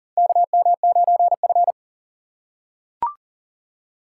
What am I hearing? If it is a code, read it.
Morse code, heard as a keyed tone, about 40 wpm, 700 Hz. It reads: KM9F
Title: USA Call Signs - 4 Characters - 3x Character Spacing - 40wpm